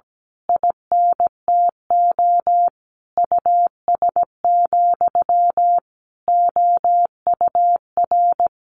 Morse code, heard as a keyed tone, about 17 wpm, 700 Hz. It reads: INTO US, OUR
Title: A Sound Of Thunder 17wpm